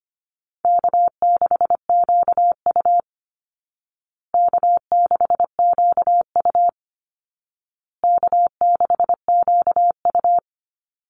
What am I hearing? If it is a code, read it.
Morse code, heard as a keyed tone, about 25 wpm, 700 Hz. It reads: K6QU K6QU K6QU
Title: Easy Call Signs Repeated 3x - 25wpm